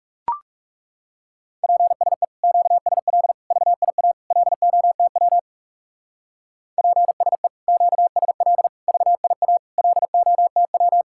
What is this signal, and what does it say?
Morse code, heard as a keyed tone, about 36 wpm, 700 Hz. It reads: PSE QSL VIA LOTW PSE QSL VIA LOTW
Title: Speed-Racing Common QSO Phrases Encore - 30wpm